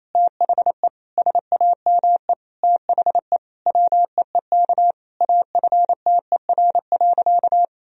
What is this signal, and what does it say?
Morse code, heard as a keyed tone, about 28 wpm, 700 Hz. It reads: THE SAME THE WEEK AFTER.
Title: Sentences from Top 500 Words and Abbreviations 28wpm